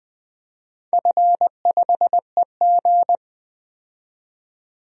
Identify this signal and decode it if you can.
Morse code, heard as a keyed tone, about 20 wpm, 700 Hz. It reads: F5EG
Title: DXCC-100 Call Signs - 4 Characters - 20wpm